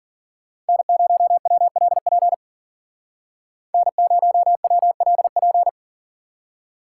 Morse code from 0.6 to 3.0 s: N0WLP

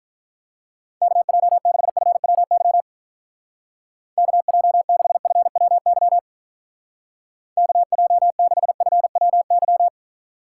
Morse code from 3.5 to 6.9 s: KJ6FWY

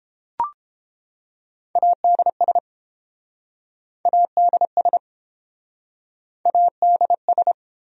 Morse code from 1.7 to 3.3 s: ADS

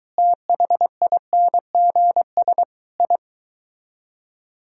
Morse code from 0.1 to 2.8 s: THINGS

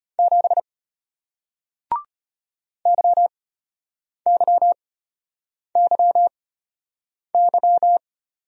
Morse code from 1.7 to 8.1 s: YYYY

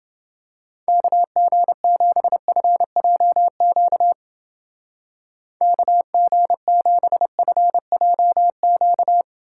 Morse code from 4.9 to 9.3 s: KG7FJQ

